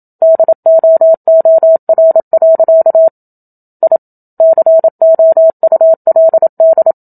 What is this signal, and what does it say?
Morse code, heard as a keyed tone, about 27 wpm, 650 Hz. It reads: DOOR. I COULD
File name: Book Two / 01-Chapter 1 - 27 WPM